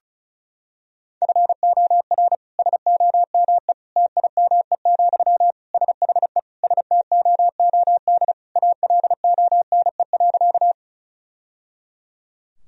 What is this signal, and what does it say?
Morse code, heard as a keyed tone, about 36 wpm, 700 Hz. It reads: FOR SOME TIME, SHE STOOD ALONE.
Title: Sentences from Top 1000 Words - Review Part 2 - 35wpm